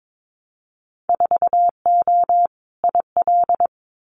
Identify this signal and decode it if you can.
Morse code, heard as a keyed tone, about 22 wpm, 700 Hz. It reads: 4O IL
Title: Winter Field Day Exchanges Encore 22wpm